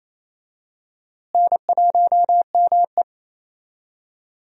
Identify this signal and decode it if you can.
Morse code, heard as a keyed tone, about 28 wpm, 700 Hz. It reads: N1ME
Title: USA Call Signs - 4 Characters - Repeated 3x - 28wpm